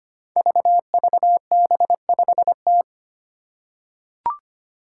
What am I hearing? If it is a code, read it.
Morse code, heard as a keyed tone, about 25 wpm, 700 Hz. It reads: VVB5T
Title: Five Challenging Letter-Number Combinations 25wpm